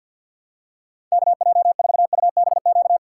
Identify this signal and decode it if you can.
Morse code, heard as a keyed tone, about 50 wpm, 700 Hz. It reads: KJ4UBQ
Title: USA Call Signs - 6 Characters - 6x Character Spacing - 50wpm